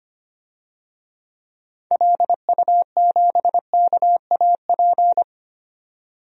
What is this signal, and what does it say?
Morse code, heard as a keyed tone, about 25 wpm, 700 Hz. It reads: LU7KAP
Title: DXCC-100 Call Signs - 6 Characters - 25wpm